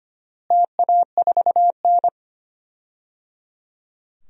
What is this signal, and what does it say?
Morse code, heard as a keyed tone, about 25 wpm, 700 Hz. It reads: TA4N
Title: DXCC-100 Call Signs - 4 Characters - Repeated 3x - 25wpm